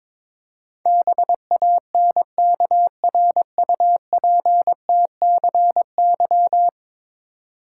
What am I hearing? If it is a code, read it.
Morse code, heard as a keyed tone, about 22 wpm, 700 Hz. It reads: BANKRUPTCY
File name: practice / 22wpm / Ten Letter Words - 22wpm